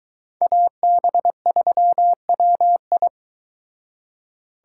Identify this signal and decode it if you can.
Morse code, heard as a keyed tone, about 22 wpm, 700 Hz. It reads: AB3WI